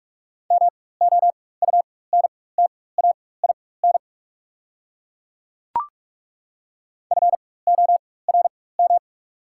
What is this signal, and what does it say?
Morse code, heard as a keyed tone, about 42 wpm, spaced as for 21 wpm, 690 Hz. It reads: MOUNTAIN FORM
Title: 4x Character Spacing - Top 300 Words - 45wpm